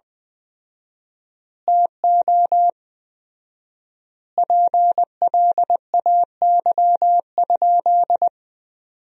Morse code, heard as a keyed tone, about 20 wpm, 700 Hz. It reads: TO PLAY?
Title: 4x Word-Spacing - Sentences from Top 400 Words - 20wpm